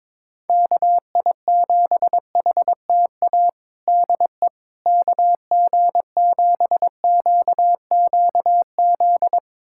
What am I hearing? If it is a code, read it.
Morse code, heard as a keyed tone, about 22 wpm, 700 Hz. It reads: KI7HTA DE KG7QQZ
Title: USA Callsign DE Callsign - 6 Characters - 22wpm